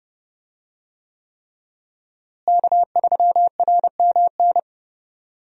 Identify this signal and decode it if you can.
Morse code, heard as a keyed tone, about 31 wpm, 700 Hz. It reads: K3RMN
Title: USA Call Signs - 5 Characters - Repeated 3x - 30wpm